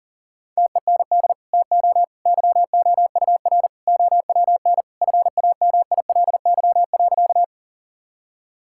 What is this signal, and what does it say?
Morse code, heard as a keyed tone, about 40 wpm, 700 Hz. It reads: TEND TO YOUR OWN FAMILY.
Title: Sentences from Top 1000 Words - Review Part 5 - 40wpm